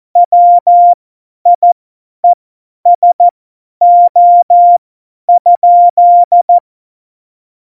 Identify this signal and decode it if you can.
Morse code, heard as a keyed tone, about 14 wpm, spaced as for 10 wpm, 700 Hz. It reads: WIESO?
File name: Morsekurs / Witze / Witz 15. Januar 2019, 50 F70